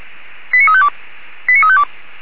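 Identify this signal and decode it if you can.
Five-tone selective call (ZVEI), heard as quick runs of five tones, tones of about 80 ms. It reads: 89352 89352
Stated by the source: ZVEI five-tone selective call received on SDR.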